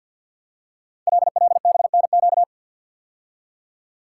Morse code, heard as a keyed tone, about 53 wpm, 700 Hz. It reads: LLBNQ
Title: Code Groups - 5 Letters - 50wpm